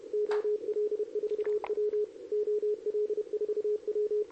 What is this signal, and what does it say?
Morse code, heard as a keyed tone, about 30 wpm, 420 Hz. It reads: OL4W OL4W
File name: CQWWCW2023 / RUN